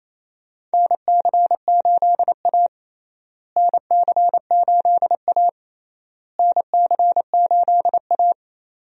Morse code, heard as a keyed tone, about 28 wpm, 700 Hz. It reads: NC8A NC8A NC8A
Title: USA Call Signs - 4 Characters - Repeated 3x - 28wpm